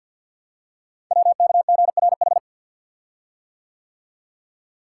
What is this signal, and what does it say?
Morse code, heard as a keyed tone, about 51 wpm, 700 Hz. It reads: WKGRH
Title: Code Groups - 5 Letters - 50wpm